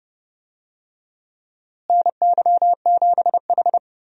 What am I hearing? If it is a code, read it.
Morse code, heard as a keyed tone, about 30 wpm, 700 Hz. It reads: NY7H
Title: USA Call Signs - 4 Characters - Repeated 2x - 30wpm